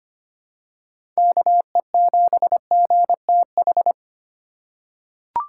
Morse code, heard as a keyed tone, about 25 wpm, 700 Hz. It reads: KE7GTH
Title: USA Call Signs - 6 Characters - 3x Character Spacing - 25wpm